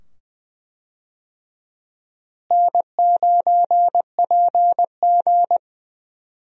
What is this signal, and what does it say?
Morse code, heard as a keyed tone, about 20 wpm, 700 Hz. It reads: N9PG
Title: USA Call Signs - 4 Characters - Repeated 2x - 20wpm